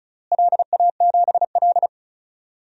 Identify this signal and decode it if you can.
Morse code, heard as a keyed tone, about 36 wpm, 700 Hz. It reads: LA7L
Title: DXCC-100 Call Signs - 4 Characters - Repeated 2x - 35wpm